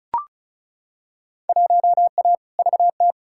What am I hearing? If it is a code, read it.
Morse code, heard as a keyed tone, about 34 wpm, 700 Hz. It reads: 1A VT
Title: Field Day Exchanges Encore - 35wpm